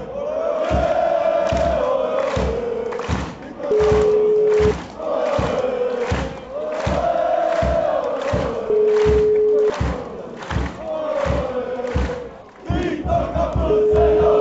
Ringback tone (Europe, 425 Hz), with noise.